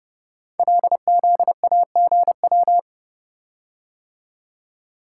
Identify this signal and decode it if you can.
Morse code, heard as a keyed tone, about 30 wpm, 700 Hz. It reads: LZAGW
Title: Code Groups - 5 Letter:Numbers - 30wpm